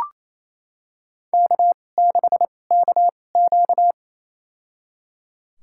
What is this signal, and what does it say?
Morse code, heard as a keyed tone, about 28 wpm, spaced as for 20 wpm, 700 Hz. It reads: K6KQ
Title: USA Call Signs - 4 Characters - 2x Character Spacing - 28wpm